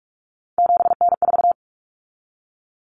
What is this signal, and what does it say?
Morse code, heard as a keyed tone, about 44 wpm, 700 Hz. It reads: 7N4
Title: Common Cut Numbers - 3-Digit 45wpm